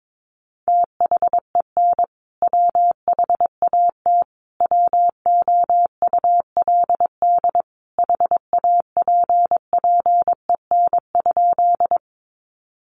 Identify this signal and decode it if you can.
Morse code, heard as a keyed tone, about 22 wpm, 700 Hz. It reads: THEN WHAT WOULD HAPPEN?
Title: Sentences from Top 400 Words 22wpm